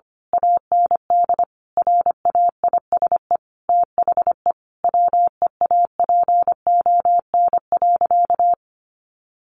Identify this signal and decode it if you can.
Morse code, heard as a keyed tone, about 25 wpm, 700 Hz. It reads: AND RAISE THE WEAPON.
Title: A Sound Of Thunder 25wpm